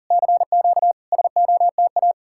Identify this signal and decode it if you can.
Morse code, heard as a keyed tone, about 52 wpm, 700 Hz. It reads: CQ SOTA